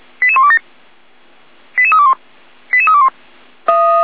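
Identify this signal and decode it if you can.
Five-tone selective call (ZVEI), heard as quick runs of five tones, tones of about 70 ms. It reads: 88127 88321 88321
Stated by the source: ZVEI five-tone call 88191 received by SDR.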